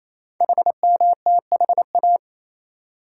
Morse code, heard as a keyed tone, about 27 wpm, 700 Hz. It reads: HMTHA
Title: Code Groups - 5 Letters - 28wpm